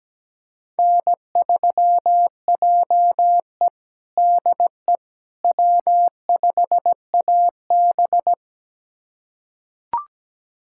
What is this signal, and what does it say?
Morse code, heard as a keyed tone, about 17 wpm, 700 Hz. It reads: N3JE DE W5AB
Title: USA Callsign DE Callsign - 4 Characters - 17wpm